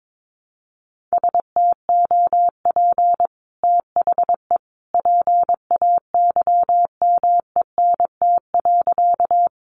Morse code, heard as a keyed tone, about 22 wpm, 700 Hz. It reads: STOP THE PAYMENT.
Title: Sentences from Top 1200 Words 22wpm